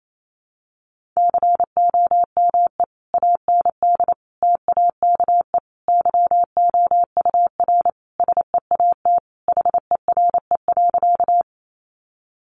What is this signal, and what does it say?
Morse code, heard as a keyed tone, about 28 wpm, 700 Hz. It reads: COME AND TAKE YOUR SEAT HERE.
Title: Sentences from Top 1500 Words 28wpm